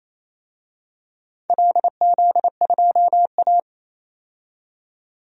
Morse code, heard as a keyed tone, about 29 wpm, 700 Hz. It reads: LZ2A
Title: DXCC-100 Call Signs - 4 Characters - 28wpm